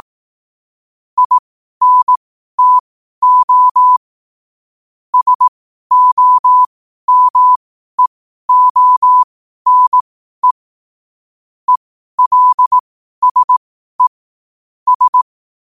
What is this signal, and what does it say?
Morse code, heard as a keyed tone, about 18 wpm, spaced as for 12 wpm, 1000 Hz. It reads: INTO SOMEONE ELSE S
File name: qotdfiles / 12wpm